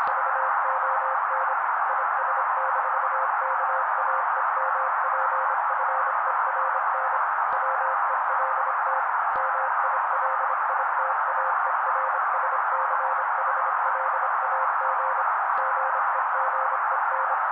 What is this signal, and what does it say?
Morse code, heard as a keyed tone, about 25 wpm, 530 Hz. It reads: VON IS-KAEMPFERN WELT MILITAERSCHLAG GEGEN
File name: daily / 25wpm noise